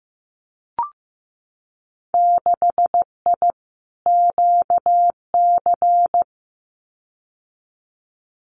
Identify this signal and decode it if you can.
Morse code, heard as a keyed tone, about 15 wpm, 700 Hz. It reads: 6I QC
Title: Winter Field Day Exchanges Encore 15wpm